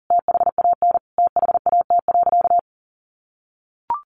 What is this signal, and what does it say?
Morse code, heard as a keyed tone, about 40 wpm, 700 Hz. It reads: THAN THAT.
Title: Sentences from Top 1200 Words 40wpm